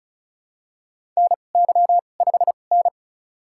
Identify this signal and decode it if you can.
Morse code, heard as a keyed tone, about 36 wpm, spaced as for 25 wpm, 700 Hz. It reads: NY5N
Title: USA Call Signs - 2x Character Spacing - 35wpm